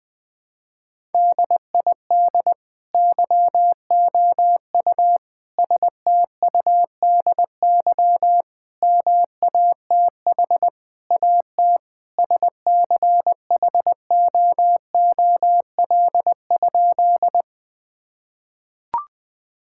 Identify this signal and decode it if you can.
Morse code, heard as a keyed tone, about 21 wpm, 700 Hz. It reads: DID YOU STUDY MATH AT SCHOOL?
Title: Sentences from Top 1000 Words - Review Part 6 - 20wpm